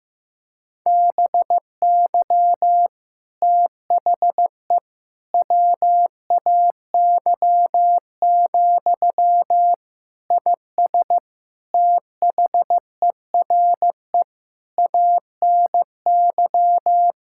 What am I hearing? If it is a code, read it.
Morse code, heard as a keyed tone, about 15 wpm, 700 Hz. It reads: BY THE WAY, IS THERE ANY
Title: Sentences from Top 1600 Words 15wpm